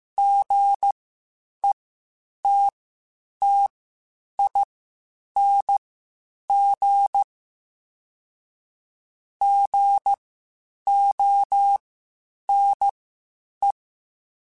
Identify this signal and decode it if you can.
Morse code, heard as a keyed tone, about 15 wpm, spaced as for 8 wpm, 800 Hz. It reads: GETTING GONE
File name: Short Runs / Short Runs-mp3s / Gotten @13